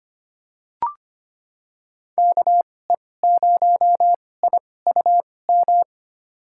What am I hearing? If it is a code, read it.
Morse code, heard as a keyed tone, about 25 wpm, spaced as for 18 wpm, 700 Hz. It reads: KE0IUM
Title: USA Call Signs - 2x Character Spacing - 25wpm